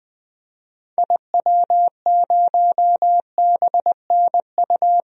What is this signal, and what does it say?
Morse code, heard as a keyed tone, about 20 wpm, 700 Hz. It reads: IW0BNU
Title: DXCC-100 Call Signs - 6 Characters - 20wpm